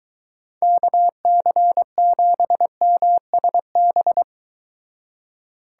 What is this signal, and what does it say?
Morse code, heard as a keyed tone, about 23 wpm, 700 Hz. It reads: KC7MSB